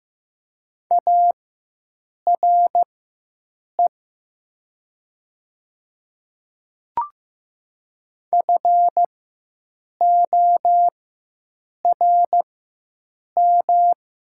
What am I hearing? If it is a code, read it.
Morse code, heard as a keyed tone, about 15 wpm, spaced as for 7 wpm, 690 Hz. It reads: ARE FORM
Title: 4x Character Spacing - Top 200 Words - 15wpm